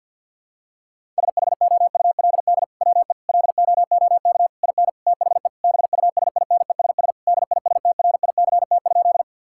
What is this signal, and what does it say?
Morse code, heard as a keyed tone, about 50 wpm, 700 Hz. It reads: SHOULD WE LOOK IN THE BUSINESS DISTRICT?
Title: Sentences from Top 1600 Words 50wpm